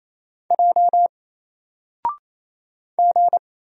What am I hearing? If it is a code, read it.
Morse code, heard as a keyed tone, about 27 wpm, 700 Hz. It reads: J G
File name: Single-Letter-Number-28wpm